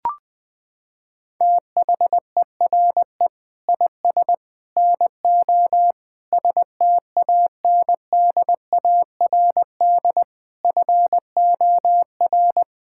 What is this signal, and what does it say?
Morse code, heard as a keyed tone, about 21 wpm, 700 Hz. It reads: THERE IS NO STANDARD FOR
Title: Sentences from Top 1600 Words 20wpm